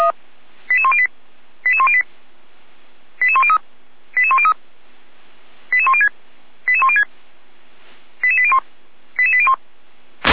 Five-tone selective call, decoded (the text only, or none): 88108 88108 88103 88103 88106 88106 88801 88801